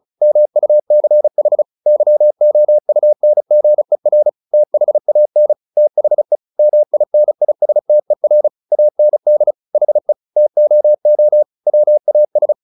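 Morse code, heard as a keyed tone, about 34 wpm, 600 Hz. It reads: MUCH YOUNGER THAN THE MINISTER AND HE TOO WAS
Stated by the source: Morse practice at 35 wpm from CW med Gnister, sending Ken Follett's novel Never in English.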